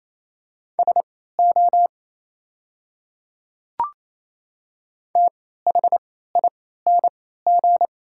Morse code, heard as a keyed tone, about 28 wpm, spaced as for 16 wpm, 690 Hz. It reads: SO THING
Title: 3x Character Spacing - Top 100 Words - 28wpm